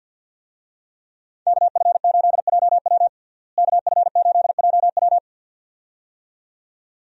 Morse code, heard as a keyed tone, about 53 wpm, 700 Hz. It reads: KF8JW KF8JW
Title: USA Call Signs - 5 Characters - Repeated 2x - 50wpm